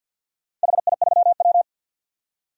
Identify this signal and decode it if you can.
Morse code, heard as a keyed tone, about 52 wpm, 700 Hz. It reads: HI3W
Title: DXCC-100 Call Signs - 4 Characters - 50wpm